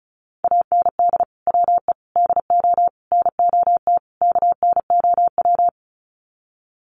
Morse code, heard as a keyed tone, about 34 wpm, 700 Hz. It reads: AND WE DO NOT KNOW
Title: Sets of 5 Words Encore 35wpm